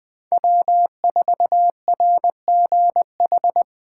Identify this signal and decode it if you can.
Morse code, heard as a keyed tone, about 20 wpm, 700 Hz. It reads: W4RGH
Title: Speed Racing - USA 1x3 Call Signs 17wpm